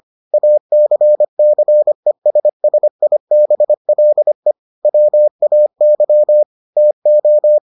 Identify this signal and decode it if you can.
Morse code, heard as a keyed tone, about 25 wpm, 600 Hz. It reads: ACCESSIBLE WAY TO